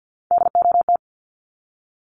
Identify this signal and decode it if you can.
Morse code, heard as a keyed tone, about 50 wpm, 700 Hz. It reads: DOT
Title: Top 1251-1500 Words 50wpm